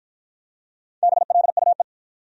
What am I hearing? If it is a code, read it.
Morse code, heard as a keyed tone, about 47 wpm, 700 Hz. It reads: BLUE